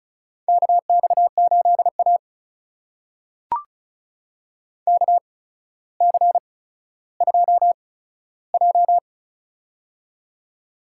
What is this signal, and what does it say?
Morse code, heard as a keyed tone, about 36 wpm, spaced as for 15 wpm, 700 Hz. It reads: KX8A KC2J
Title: USA Call Signs - 4 Characters - 8x Character Spacing - 35wpm